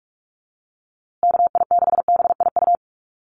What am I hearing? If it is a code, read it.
Morse code, heard as a keyed tone, about 44 wpm, 700 Hz. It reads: KI6BIU
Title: USA Call Signs Repeated 3x 45wpm